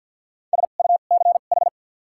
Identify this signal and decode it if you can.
Morse code, heard as a keyed tone, about 52 wpm, spaced as for 36 wpm, 700 Hz. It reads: SUCH
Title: 2x Character Spacing - Top 500 Words - 50wpm